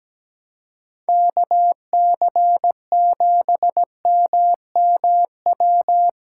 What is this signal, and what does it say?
Morse code, heard as a keyed tone, about 17 wpm, 700 Hz. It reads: KC7MMW
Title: USA Call Signs - 6 Characters - 2x Character Spacing - 17wpm